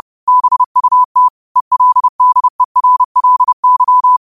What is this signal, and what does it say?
Morse code, heard as a keyed tone, about 30 wpm, 1000 Hz. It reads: BUT ELDERLY
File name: qotdfiles / 30wpm